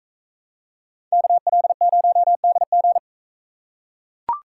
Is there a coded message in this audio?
Yes: Morse code from 1.1 to 3.0 s, heard as a keyed tone, about 38 wpm, 700 Hz. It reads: KL0DG